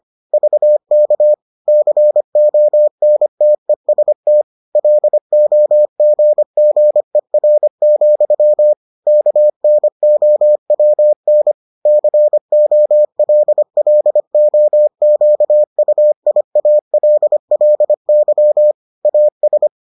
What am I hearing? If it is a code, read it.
Morse code, heard as a keyed tone, about 25 wpm, 600 Hz. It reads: VK CONTEST LOGGER, KNOWN COLLOQUIALLY AS